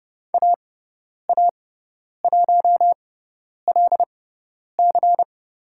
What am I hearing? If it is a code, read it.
Morse code, heard as a keyed tone, about 30 wpm, spaced as for 10 wpm, 700 Hz. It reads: AA1LC
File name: Lesson 16g - Letter c - Call Signs 30wpm